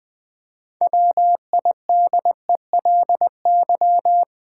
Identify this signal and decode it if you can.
Morse code, heard as a keyed tone, about 20 wpm, 700 Hz. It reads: WIDELY
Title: 6 Letter Words - First 3 Letters Given - 20wpm